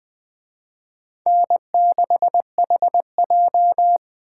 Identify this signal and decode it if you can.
Morse code, heard as a keyed tone, about 21 wpm, 700 Hz. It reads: N6HJ
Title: Speed Racing - USA 1x2 Call Signs 17wpm